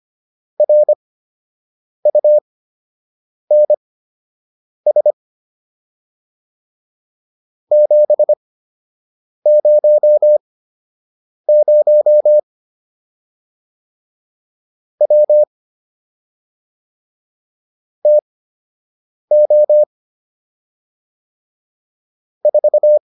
Morse code, heard as a keyed tone, about 25 wpm, spaced as for 7 wpm, 600 Hz. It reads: RUNS 700 W TO 4